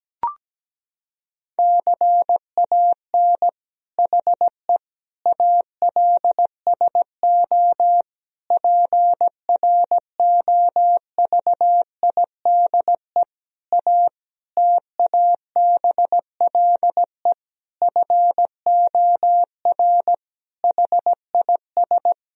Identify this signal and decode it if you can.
Morse code, heard as a keyed tone, about 17 wpm, 700 Hz. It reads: CAN HE ALSO PROVIDE A TABLE FOR HIS
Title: Sentences from Top 1000 Words - Review Part 3 - 17wpm